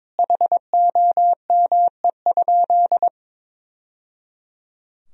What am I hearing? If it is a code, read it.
Morse code, heard as a keyed tone, about 22 wpm, 700 Hz. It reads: HOME?
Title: Sentences from Top 1000 Words - Review Part 1 - 22wpm